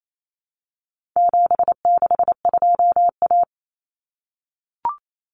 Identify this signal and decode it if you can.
Morse code, heard as a keyed tone, about 28 wpm, 700 Hz. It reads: 762A
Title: Common Cut Numbers - 4-Digit 28wpm